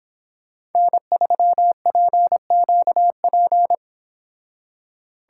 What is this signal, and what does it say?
Morse code, heard as a keyed tone, about 30 wpm, 700 Hz. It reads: N3PQP